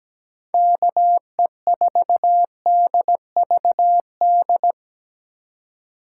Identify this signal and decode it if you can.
Morse code, heard as a keyed tone, about 20 wpm, 700 Hz. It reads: KE4DVD
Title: Speed Racing - USA 2x3 Call Signs 17wpm